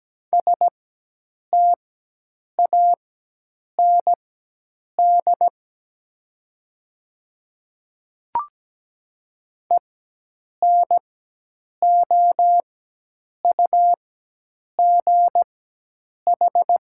Morse code, heard as a keyed tone, about 17 wpm, spaced as for 8 wpm, 700 Hz. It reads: STAND ENOUGH
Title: 4x Character Spacing - Top 500 Words - 17wpm